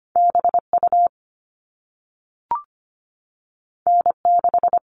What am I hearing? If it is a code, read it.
Morse code, heard as a keyed tone, about 25 wpm, 700 Hz. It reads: BU N6
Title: All Cut Numbers - 2-Digit 25wpm